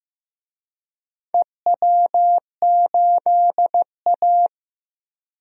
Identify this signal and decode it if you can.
Morse code, heard as a keyed tone, about 15 wpm, 700 Hz. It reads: EW8A
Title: DXCC-100 Call Signs - 4 Characters - Repeated 2x - 15wpm